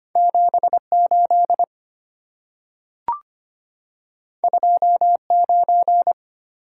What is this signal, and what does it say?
Morse code, heard as a keyed tone, about 25 wpm, 690 Hz. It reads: 78 29
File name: Two Letter-Number Combinations - 25wpm